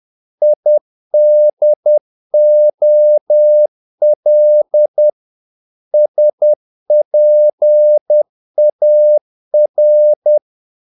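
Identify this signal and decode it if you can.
Morse code, heard as a keyed tone, about 10 wpm, 600 Hz. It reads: IDOL SPAR
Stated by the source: Morse practice of short Danish words at 10 wpm, from CW med Gnister.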